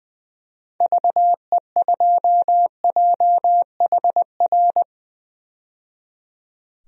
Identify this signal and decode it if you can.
Morse code, heard as a keyed tone, about 20 wpm, 700 Hz. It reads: VE2JHR
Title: DXCC-100 Call Signs - 6 Characters - 20wpm